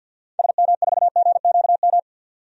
Morse code, heard as a keyed tone, about 52 wpm, 700 Hz. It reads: SM4GQM
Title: DXCC-100 Call Signs - 6 Characters - 50wpm